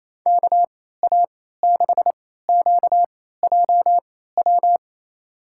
Morse code, heard as a keyed tone, about 28 wpm, spaced as for 16 wpm, 700 Hz. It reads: KA6QJW
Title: USA Call Signs - 3x Character Spacing - 28wpm